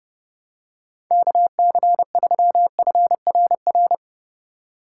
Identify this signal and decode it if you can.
Morse code, heard as a keyed tone, about 30 wpm, 700 Hz. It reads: KC3FRR
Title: USA Call Signs - 6 Characters - 2x Character Spacing - 30wpm